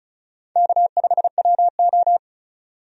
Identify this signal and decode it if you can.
Morse code, heard as a keyed tone, about 36 wpm, 700 Hz. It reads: K5WO
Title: USA Call Signs - 4 Characters - 35wpm